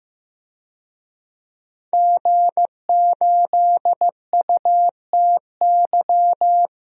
Morse code, heard as a keyed tone, about 15 wpm, 700 Hz. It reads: G8UTY
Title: DXCC-100 Call Signs - 5 Characters - 15wpm